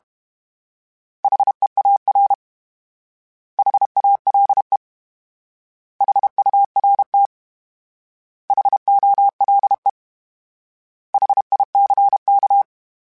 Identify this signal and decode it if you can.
Morse code, heard as a keyed tone, about 32 wpm, 800 Hz. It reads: HEAR HALE HURT HOLE HICK
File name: H Prefixes / H mp3s